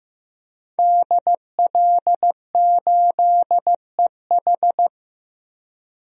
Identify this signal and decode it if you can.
Morse code, heard as a keyed tone, about 15 wpm, 700 Hz. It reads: DL8EH
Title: DXCC-100 Call Signs - 5 Characters - Repeated 2x - 15wpm